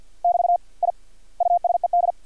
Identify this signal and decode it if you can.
Morse code, heard as a keyed tone, about 50 wpm, 700 Hz. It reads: <BT> I USED
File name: archived / CRSnet